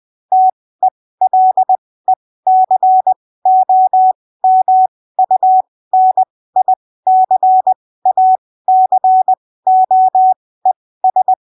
Morse code, heard as a keyed tone, about 20 wpm, spaced as for 15 wpm, 750 Hz. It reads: TELECOMUNICACOES
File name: arquivos / texto / texto 22 20ppm